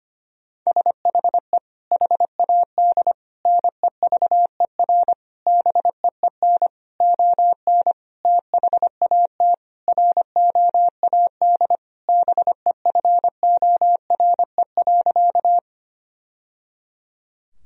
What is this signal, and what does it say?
Morse code, heard as a keyed tone, about 25 wpm, 700 Hz. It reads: SHE HAD NEVER BEEN ON THAT ROAD BEFORE.
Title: Sentences from Top 1000 Words - Review Part 2 - 25wpm